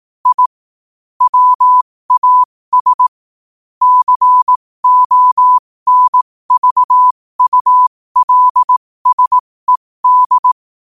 Morse code, heard as a keyed tone, about 18 wpm, spaced as for 15 wpm, 1000 Hz. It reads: I WAS CONVULSED
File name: qotdfiles / 15wpm